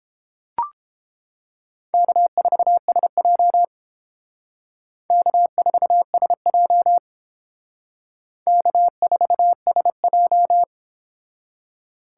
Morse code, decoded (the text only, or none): K4SJ K4SJ K4SJ